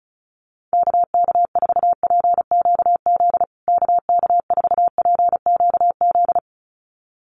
Morse code, heard as a keyed tone, about 34 wpm, 700 Hz. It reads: KK4PQZ KK4PQZ
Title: USA Call Signs Repeated 2x 35wpm